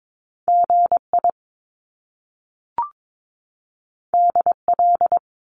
Morse code, heard as a keyed tone, about 22 wpm, 700 Hz. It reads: GI DL
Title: Top Two Letter Combinations in Words 22wpm